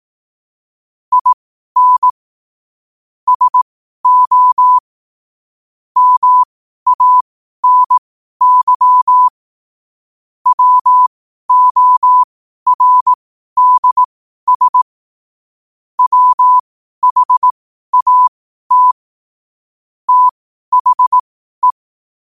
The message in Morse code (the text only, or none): IN SO MANY WORDS WHAT THE